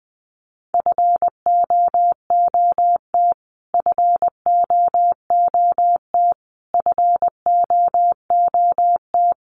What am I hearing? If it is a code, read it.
Morse code, heard as a keyed tone, about 20 wpm, 700 Hz. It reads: FOOT FOOT FOOT
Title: Top 500 Words Repeated 20wpm